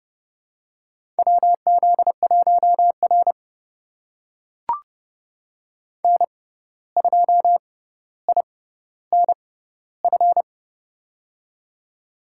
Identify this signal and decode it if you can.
Morse code, heard as a keyed tone, about 30 wpm, spaced as for 14 wpm, 700 Hz. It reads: WZ1R N2INF
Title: USA Call Signs - 6x Character Spacing - 30wpm